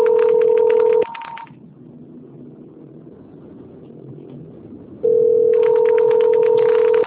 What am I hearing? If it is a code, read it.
Ringback tone (North America), 440 + 480 Hz.
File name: out- - - - - .41769